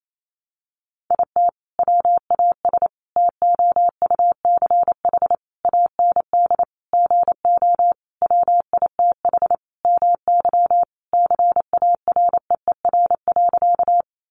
Morse code, heard as a keyed tone, about 28 wpm, 700 Hz. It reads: IT WAS TOUCH AND GO WITH MY CAREER.
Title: Sentences from Top 1200 Words 28wpm